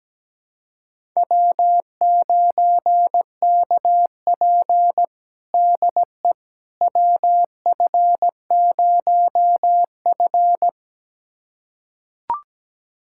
Morse code, heard as a keyed tone, about 17 wpm, 700 Hz. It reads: W9KP DE WF0F
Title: USA Callsign DE Callsign - 4 Characters - 17wpm